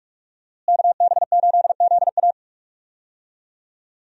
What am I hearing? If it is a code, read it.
Morse code, heard as a keyed tone, about 44 wpm, 700 Hz. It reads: KB8ZA
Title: USA Call Signs - 5 Characters - Repeated 3x - 45wpm